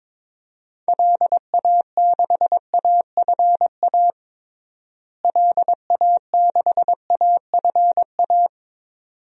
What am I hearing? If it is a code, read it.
Morse code, heard as a keyed tone, about 21 wpm, 700 Hz. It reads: LA6AFA LA6AFA
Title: DXCC-100 Call Signs - 6 Characters - Repeated 2x - 22wpm